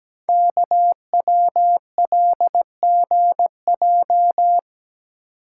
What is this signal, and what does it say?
Morse code, heard as a keyed tone, about 17 wpm, 700 Hz. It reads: KWLGJ
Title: Code Groups - 5 Letters - 17wpm